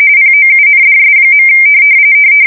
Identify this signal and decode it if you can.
RTTY, about 45.45 baud. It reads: POUNDS.
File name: fsk170 / teletype-corp